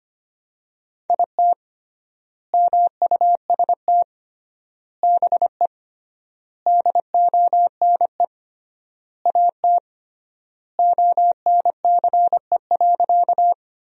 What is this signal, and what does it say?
Morse code, heard as a keyed tone, about 25 wpm, 700 Hz. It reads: IT MUST BE DONE AT ONCE.
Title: 3x Word-Spacing - Sentences from Top 500 Words - 25wpm